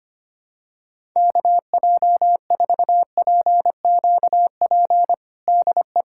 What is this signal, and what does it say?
Morse code, heard as a keyed tone, about 25 wpm, 700 Hz. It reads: KJ4PQP DE
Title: USA Callsign DE Callsign - 6 Characters - 25wpm